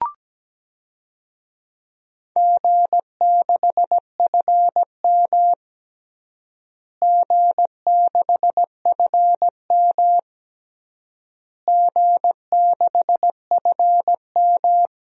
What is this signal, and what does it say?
Morse code, heard as a keyed tone, about 17 wpm, 700 Hz. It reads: G6FM G6FM G6FM
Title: DXCC-100 Call Signs - 4 Characters - Repeated 3x - 17wpm